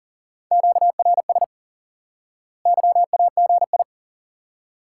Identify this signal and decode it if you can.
Morse code, heard as a keyed tone, about 37 wpm, 700 Hz. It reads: QRS YAGI
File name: IQE Course 3 - Lesson 89 - YAGI - Two Elements - 40wpm